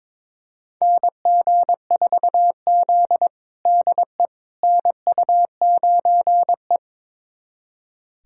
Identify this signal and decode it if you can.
Morse code, heard as a keyed tone, about 22 wpm, 700 Hz. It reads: NG4Z DE NU9E
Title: USA Callsign DE Callsign - 4 Characters - 22wpm